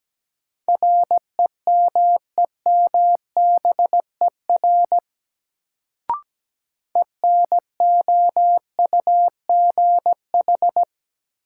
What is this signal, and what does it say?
Morse code, heard as a keyed tone, about 17 wpm, 700 Hz. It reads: REMEMBER ENOUGH
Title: Sentences from Top 400 Words - New Word Drill - 17wpm